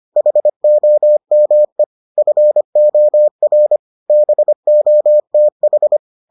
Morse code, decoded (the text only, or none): HOME FOR BOTH